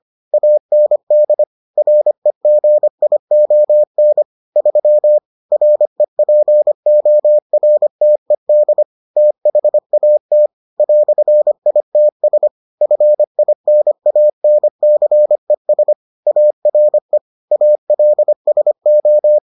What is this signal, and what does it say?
Morse code, heard as a keyed tone, about 25 wpm, 600 Hz. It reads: AND REGION 3 REPORTED THAT "ITS FINANCES ARE ALSO